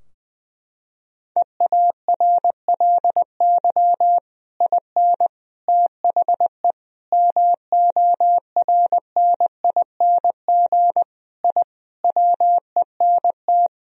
Morse code, heard as a keyed tone, about 20 wpm, 700 Hz. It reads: EARLY IN THE MORNING I WENT
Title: Sentences from Top 1000 Words - Review Part 2 - 20wpm